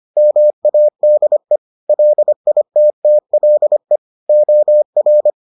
Morse code, heard as a keyed tone, about 25 wpm, 600 Hz. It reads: MADE LITTLE OR